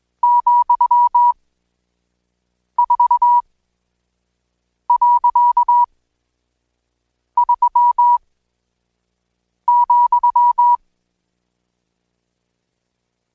Morse code, heard as a keyed tone, about 20 wpm, spaced as for 5 wpm, 970 Hz. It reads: ,4.3,
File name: Code Quick / AudioDownld1 / Seg76-Day8-DayDrill 8